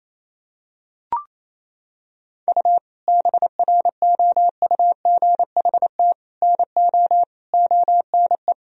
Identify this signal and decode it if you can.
Morse code, heard as a keyed tone, about 28 wpm, 700 Hz. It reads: U BROUGHT NO ONE
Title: Sentences from Top 500 Words and Abbreviations 28wpm